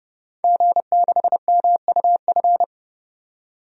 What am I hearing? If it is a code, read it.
Morse code, heard as a keyed tone, about 30 wpm, 700 Hz. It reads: G6MUF
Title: DXCC-100 Call Signs - 5 Characters - 30wpm